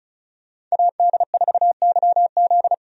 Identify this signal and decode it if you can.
Morse code, heard as a keyed tone, about 36 wpm, 700 Hz. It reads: AD4YZ
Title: USA Call Signs - 5 Characters - 35wpm